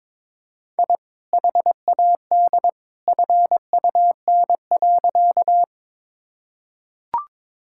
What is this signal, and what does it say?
Morse code, heard as a keyed tone, about 22 wpm, 700 Hz. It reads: I HAD FUN.
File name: practice / 22wpm / Sentences with 2-Word Phrases - 22wpm